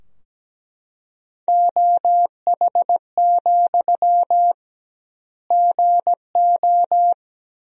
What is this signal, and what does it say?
Morse code, heard as a keyed tone, about 17 wpm, 700 Hz. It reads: OH, GO
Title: 2x Word-Spacing - Sentences from Top 500 Words - 17wpm